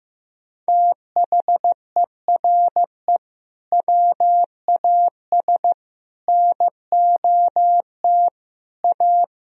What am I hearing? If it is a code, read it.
Morse code, heard as a keyed tone, about 15 wpm, 700 Hz. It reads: THERE WAS NOT A
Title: Sentences from Top 1000 Words - Review Part 4 - 15wpm